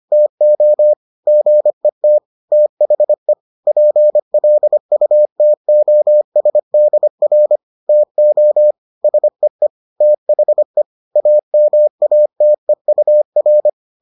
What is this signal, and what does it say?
Morse code, heard as a keyed tone, about 25 wpm, 600 Hz. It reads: TO GET THE PLUTOSDR TO SEE THE AMATEUR